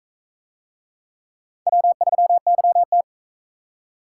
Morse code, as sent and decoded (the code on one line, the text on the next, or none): .-- ...-- -.-- -
W3YT